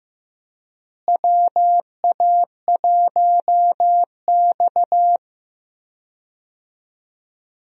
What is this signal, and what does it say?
Morse code, heard as a keyed tone, about 15 wpm, 700 Hz. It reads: WA1X
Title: USA Call Signs - 4 Characters - Repeated 2x - 15wpm